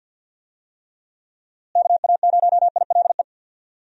Morse code, heard as a keyed tone, about 45 wpm, 700 Hz. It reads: KA0ILE
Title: USA Call Signs - 6 Characters - Repeated 2x - 50wpm